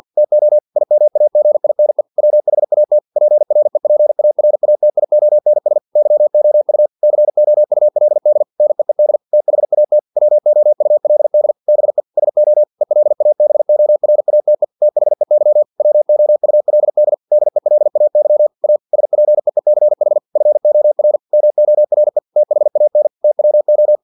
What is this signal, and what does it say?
Morse code, heard as a keyed tone, about 50 wpm, 600 Hz. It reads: TO IMAGINE WHAT PREPARATIONS YOU COULD NEED THAT WOULD BE SO ELABORATE THEY WOULD DELAY A SPEECH FOR MORE THAN TWO
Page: Never 49wpm